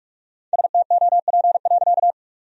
Morse code, heard as a keyed tone, about 46 wpm, 700 Hz. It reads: STOP.
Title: Sentences from Top 1000 Words - Review Part 1 - 45wpm